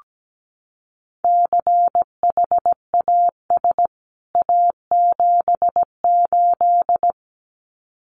Morse code, heard as a keyed tone, about 17 wpm, 700 Hz. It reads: CHAS A78
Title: CWT Exchange 17wpm